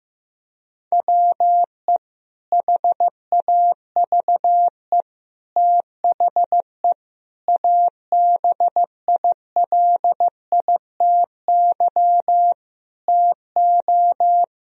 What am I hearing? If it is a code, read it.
Morse code, heard as a keyed tone, about 16 wpm, 700 Hz. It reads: WE HAVE THE ABILITY TO
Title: Sentences from Top 1000 Words - Review Part 3 - 15wpm